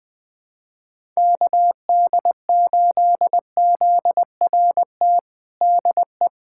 Morse code, heard as a keyed tone, about 20 wpm, 700 Hz. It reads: KD8ZRT DE
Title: USA Callsign DE Callsign 20wpm